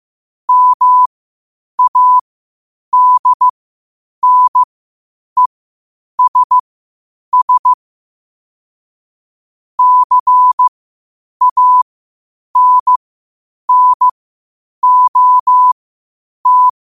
Morse code, heard as a keyed tone, about 15 wpm, spaced as for 8 wpm, 1000 Hz. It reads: MADNESS CANNOT
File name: qotdfiles / 8wpm